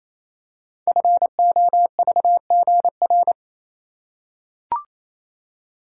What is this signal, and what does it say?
Morse code, heard as a keyed tone, about 27 wpm, 700 Hz. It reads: FOVGR
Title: Code Groups - 5 Letters - 28wpm